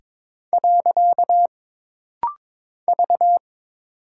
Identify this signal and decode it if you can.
Morse code, heard as a keyed tone, about 22 wpm, 700 Hz. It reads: . V
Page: This rapid-fire timing is geared to push your ICR (Instant Character Recognition) skills.